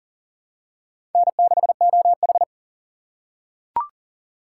Morse code, heard as a keyed tone, about 40 wpm, 700 Hz. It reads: N6OH
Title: USA Call Signs - 4 Characters - 4x Character Spacing - 40wpm